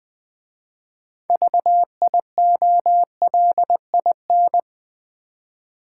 Morse code, heard as a keyed tone, about 20 wpm, 700 Hz. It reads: VIOLIN
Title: Easy Words with Hints - 20wpm